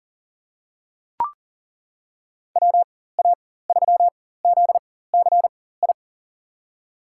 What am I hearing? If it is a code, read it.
Morse code, heard as a keyed tone, about 40 wpm, spaced as for 19 wpm, 690 Hz. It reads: WA3ZCI
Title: USA Call Signs - 4x Character Spacing - 40wpm